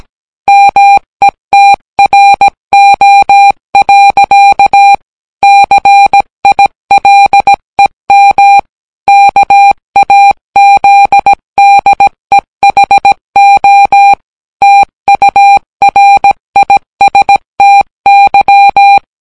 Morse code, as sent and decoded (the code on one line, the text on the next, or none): -- . - .-. --- .-.-.- / -.-. .. .-.. -- / -.- .- --.. -.. .... --- / - ..- .-. .. ... - -.--
METRO. CILEM KAZDEHO TURISTY